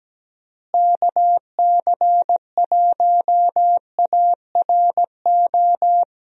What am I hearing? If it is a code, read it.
Morse code, heard as a keyed tone, about 17 wpm, 700 Hz. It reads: KC1ARO
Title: USA Call Signs - 6 Characters - Repeated 3x - 17wpm